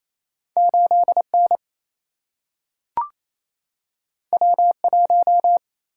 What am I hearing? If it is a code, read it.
Morse code, heard as a keyed tone, about 28 wpm, 700 Hz. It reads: 8N W1
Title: Single-Letter Single-Number 28wpm